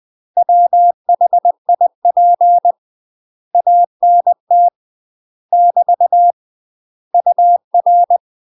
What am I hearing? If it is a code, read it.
Morse code, heard as a keyed tone, about 20 wpm, 700 Hz. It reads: WHIP ANT <BT> UR
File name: UK MASTER QSOs TRAINING / 20WPM / UK MASTER QSOs 0030